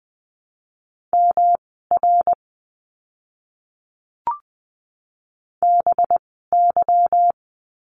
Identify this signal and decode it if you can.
Morse code, heard as a keyed tone, about 20 wpm, spaced as for 15 wpm, 700 Hz. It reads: MR BY
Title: 244 Lesson 39f - Letter z - Words 20wpm